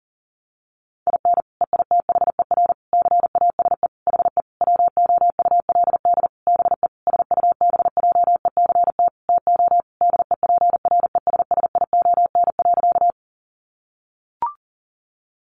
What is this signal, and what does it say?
Morse code, heard as a keyed tone, about 40 wpm, 700 Hz. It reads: IN EITHER CASE HE WOULD BE SUBJECT TO DEPRESSION.
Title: Sentences from Top 1300 Words 40wpm